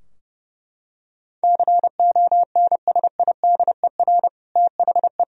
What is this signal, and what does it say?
Morse code, heard as a keyed tone, about 30 wpm, 700 Hz. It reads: CONSIDER THE
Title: Sentences from Top 1000 Words - Review Part 4 - 30wpm